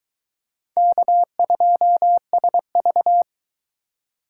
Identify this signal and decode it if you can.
Morse code, heard as a keyed tone, about 24 wpm, 700 Hz. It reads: K2SV